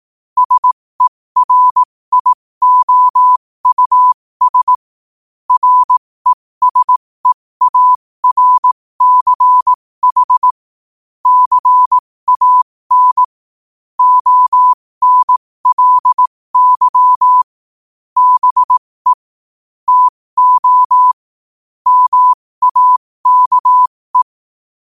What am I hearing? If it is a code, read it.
Morse code, heard as a keyed tone, about 18 wpm, spaced as for 15 wpm, 1000 Hz. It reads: SERIOUS RESEARCH CAN ONLY BE TO MAKE
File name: qotdfiles / 15wpm